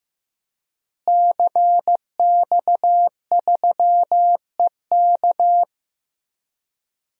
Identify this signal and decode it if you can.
Morse code, heard as a keyed tone, about 15 wpm, 700 Hz. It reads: CX3EK
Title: DXCC-100 Call Signs - 5 Characters - Repeated 2x - 15wpm